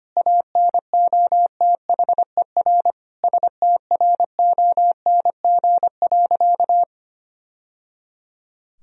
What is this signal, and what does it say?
Morse code, heard as a keyed tone, about 25 wpm, 700 Hz. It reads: ANOTHER STRONG.
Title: Sentences from Top 1000 Words - Review Part 3 - 25wpm